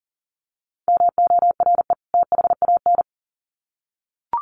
Morse code, heard as a keyed tone, about 40 wpm, 700 Hz. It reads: MORE THAN
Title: Sets of 2 Words 40wpm